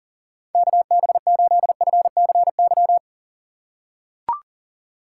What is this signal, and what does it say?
Morse code, heard as a keyed tone, about 40 wpm, 700 Hz. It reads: KB8FCY
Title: USA Call Signs - 6 Characters - 3x Character Spacing - 40wpm